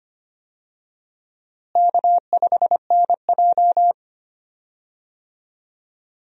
Morse code, heard as a keyed tone, about 25 wpm, 700 Hz. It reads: K5NJ
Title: USA Call Signs - 4 Characters - Repeated 3x - 25wpm